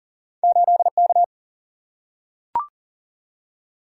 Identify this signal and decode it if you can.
Morse code, heard as a keyed tone, about 40 wpm, 700 Hz. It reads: 8K
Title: Single-Letter Single-Number 40wpm